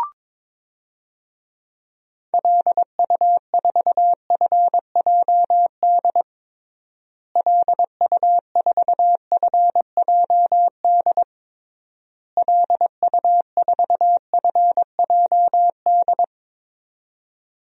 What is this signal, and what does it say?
Morse code, heard as a keyed tone, about 22 wpm, 700 Hz. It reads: LU4FJD LU4FJD LU4FJD
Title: DXCC-100 Call Signs - 6 Characters - Repeated 3x - 22wpm